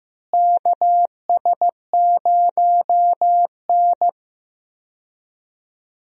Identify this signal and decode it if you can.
Morse code, heard as a keyed tone, about 15 wpm, 700 Hz. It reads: KS0N
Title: USA Call Signs - 4 Characters - Repeated 2x - 15wpm